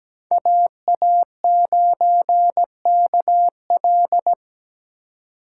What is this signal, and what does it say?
Morse code, heard as a keyed tone, about 17 wpm, 700 Hz. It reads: AA9KL
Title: USA Call Signs - 5 Characters - 8x Character Spacing - 17wpm